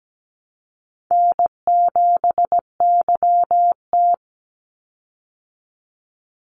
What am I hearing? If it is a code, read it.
Morse code, heard as a keyed tone, about 17 wpm, 700 Hz. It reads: N7YT
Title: Call Signs Repeated 3x - 17wpm